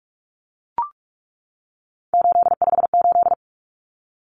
Morse code, heard as a keyed tone, about 44 wpm, 700 Hz. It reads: 858
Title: Numbers - 3-Digit 45wpm